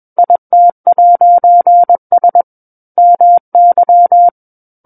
Morse code, heard as a keyed tone, about 21 wpm, 700 Hz. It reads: IT'S MY
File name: Book One / 12-Chapter 12 - 21 WPM